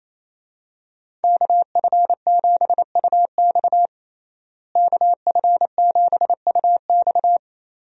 Morse code, heard as a keyed tone, about 28 wpm, 700 Hz. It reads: KF7UX KF7UX
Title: USA Call Signs - 5 Characters - Repeated 3x - 28wpm